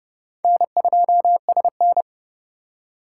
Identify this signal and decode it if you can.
Morse code, heard as a keyed tone, about 36 wpm, 700 Hz. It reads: N2SN
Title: Speed Racing - USA 1x2 Call Signs 30wpm